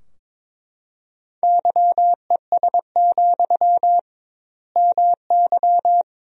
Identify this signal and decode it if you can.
Morse code, heard as a keyed tone, about 22 wpm, 700 Hz. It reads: YES, MY
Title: 2x Word-Spacing - Sentences from Top 500 Words - 22wpm